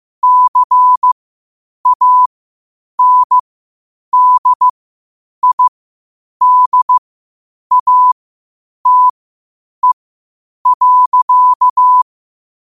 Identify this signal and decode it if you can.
Morse code, heard as a keyed tone, about 15 wpm, spaced as for 8 wpm, 1000 Hz. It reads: CANDIDATE.
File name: qotdfiles / 8wpm